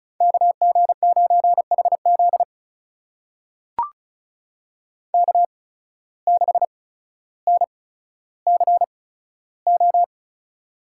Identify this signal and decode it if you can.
Morse code, heard as a keyed tone, about 36 wpm, spaced as for 15 wpm, 700 Hz. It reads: KG9HZ K6NCO
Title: USA Call Signs - 5 Characters - 8x Character Spacing - 35wpm